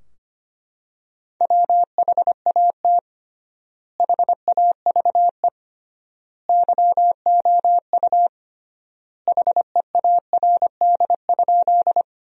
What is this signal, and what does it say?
Morse code, heard as a keyed tone, about 25 wpm, 700 Hz. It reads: WHAT HAVE YOU HEARD?
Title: 3x Word-Spacing - Sentences from Top 400 Words - 25wpm